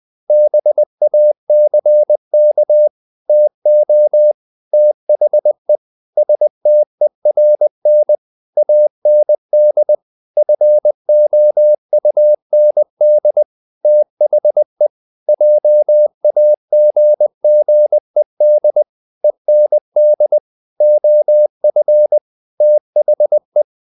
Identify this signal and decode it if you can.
Morse code, heard as a keyed tone, about 20 wpm, 600 Hz. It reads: BACK TO THE STERN AND FOUND THE JAGGED END OF THE
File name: the old man and the sea part 2 20wpm